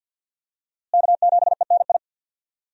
Morse code, heard as a keyed tone, about 53 wpm, 700 Hz. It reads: K7ENI
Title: USA Call Signs - 5 Characters - Repeated 2x - 50wpm